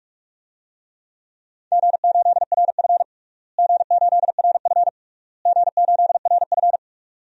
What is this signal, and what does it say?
Morse code, heard as a keyed tone, about 46 wpm, 700 Hz. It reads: G8RF G8RF G8RF
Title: DXCC-100 Call Signs - 4 Characters - Repeated 3x - 45wpm